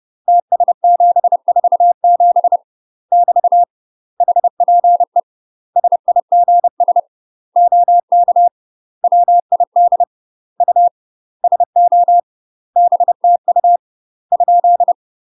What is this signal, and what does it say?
Morse code, heard as a keyed tone, about 30 wpm, 700 Hz. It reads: TS747 <BT> HPE SIGS OK WID U SO BTU ?
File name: UK MASTER QSOs TRAINING / 30WPM / UK MASTER QSOs 0019